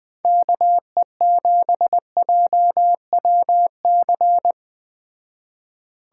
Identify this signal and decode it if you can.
Morse code, heard as a keyed tone, about 20 wpm, 700 Hz. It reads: KE7JWC